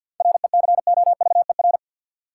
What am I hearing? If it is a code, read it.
Morse code, heard as a keyed tone, about 53 wpm, 700 Hz. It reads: RECOVER
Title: Seven Letter Words - 50wpm